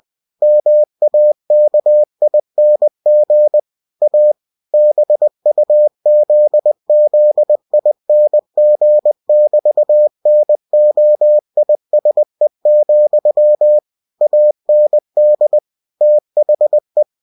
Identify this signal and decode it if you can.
Morse code, heard as a keyed tone, about 20 wpm, 600 Hz. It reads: MAKING A BUZZING<BT>NOISE, AND THE
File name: cw-books / winnie-the-pooh / winnie-20wpm-page-0002